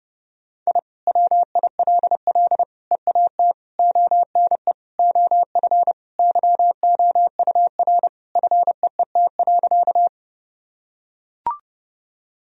Morse code, heard as a keyed tone, about 30 wpm, 700 Hz. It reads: I WILL EAT ONE OF YOUR FEET.
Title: Sentences from Top 1000 Words - Review Part 1 - 30wpm